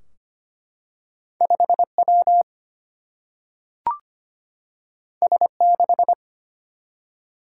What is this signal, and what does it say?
Morse code, heard as a keyed tone, about 25 wpm, 700 Hz. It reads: 5W S6
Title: Single-Letter Single-Number 25wpm